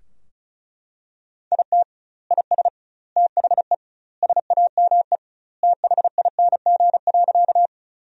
Morse code, heard as a keyed tone, about 36 wpm, 700 Hz. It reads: IT IS THE SAME THING.
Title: 2x Word-Spacing - Sentences from Top 200 Words - 35wpm